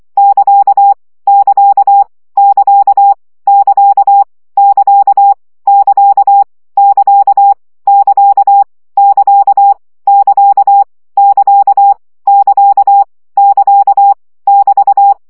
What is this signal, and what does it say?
Morse code, heard as a keyed tone, about 24 wpm, 800 Hz. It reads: <KA> <KA> <KA> <KA> <KA> <KA> <KA> <KA> <KA> <KA> <KA> <KA> <KA> <BT>
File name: CW gyakorlas ct 20 baud